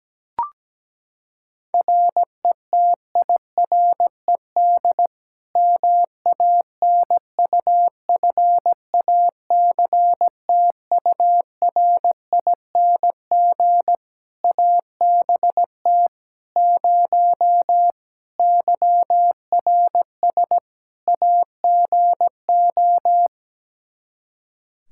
Morse code, heard as a keyed tone, about 17 wpm, 700 Hz. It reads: RETIRED MANUFACTURING ABT 0 YRS AGO
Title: Practice QSO - Exchange 3 - More Info - 17wpm